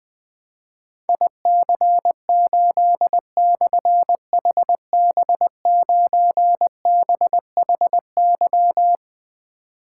Morse code, heard as a keyed tone, about 21 wpm, 700 Hz. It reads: IC8/HB9BHY
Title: Difficult Call Signs 20wpm